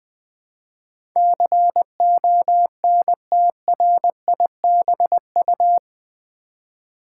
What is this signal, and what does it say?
Morse code, heard as a keyed tone, about 20 wpm, 700 Hz. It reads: CONTRIBU
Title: Build A Word - 4 Letters At A Time - 10-Letter Words - 20wpm